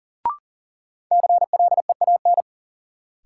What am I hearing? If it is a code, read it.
Morse code, heard as a keyed tone, about 38 wpm, 700 Hz. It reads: CLEAN
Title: Sentences from Top 1100 Words - New Word Drill - 40wpm